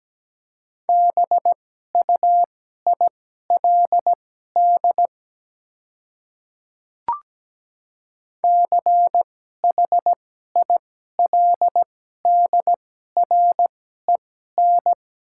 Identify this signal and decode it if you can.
Morse code, heard as a keyed tone, about 17 wpm, spaced as for 12 wpm, 700 Hz. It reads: BUILD CHILDREN
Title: 2x Character Spacing - Top 300 Words - 17wpm